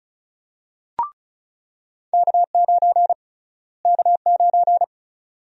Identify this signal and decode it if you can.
Morse code, heard as a keyed tone, about 36 wpm, 700 Hz. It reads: K9 K9
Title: USA Call Sign Prefixes - Repeated 2x - 35wpm